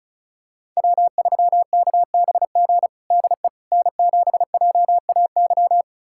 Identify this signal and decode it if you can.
Morse code, heard as a keyed tone, about 36 wpm, 700 Hz. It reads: W3KBG DE N7JAY
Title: USA Callsign DE Callsign - 5 Characters - 35wpm